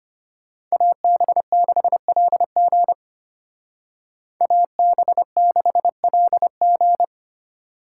Morse code, heard as a keyed tone, about 32 wpm, 700 Hz. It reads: AB6LG AB6LG